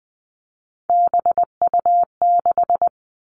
Morse code, heard as a keyed tone, about 20 wpm, 700 Hz. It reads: BU6
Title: Three Challenging Letter-Number Combinations 20wpm